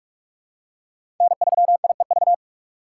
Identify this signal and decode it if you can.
Morse code, heard as a keyed tone, about 44 wpm, 700 Hz. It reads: N3IEV
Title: USA Call Signs - 2x Character Spacing - 45wpm